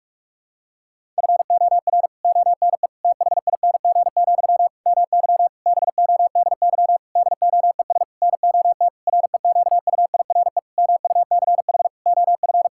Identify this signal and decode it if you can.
Morse code, heard as a keyed tone, about 46 wpm, 700 Hz. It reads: FOR ONE THING, MY BODY DOES NOT REQUIRE MUCH OF
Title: Sentences from Top 1000 Words - Review Part 6 - 45wpm